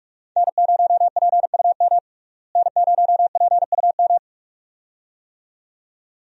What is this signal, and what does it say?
Morse code, heard as a keyed tone, about 42 wpm, 700 Hz. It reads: N0PUM N0PUM
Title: USA Call Signs - 5 Characters - Repeated 2x - 45wpm